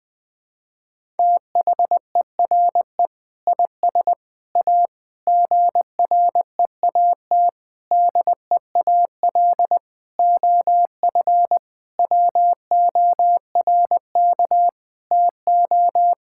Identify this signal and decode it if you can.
Morse code, heard as a keyed tone, about 20 wpm, 700 Hz. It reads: THERE IS A GREAT DEAL OF WORK TO
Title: Sentences with 6-Word Phrases - 20wpm